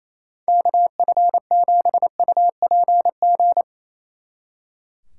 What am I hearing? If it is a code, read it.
Morse code, heard as a keyed tone, about 35 wpm, 700 Hz. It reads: KF7UPG
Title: Speed Racing - USA 2x3 Call Signs 28wpm